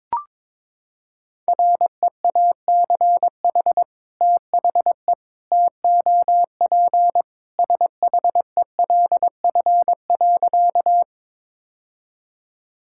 Morse code, heard as a keyed tone, about 22 wpm, 700 Hz. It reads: REACH THE TOP SHELF.
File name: practice / 22wpm / Sentences with 2-Word Phrases - 22wpm